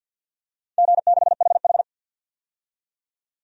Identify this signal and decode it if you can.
Morse code, heard as a keyed tone, about 48 wpm, 700 Hz. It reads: G6HH
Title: DXCC-100 Call Signs - 4 Characters - 50wpm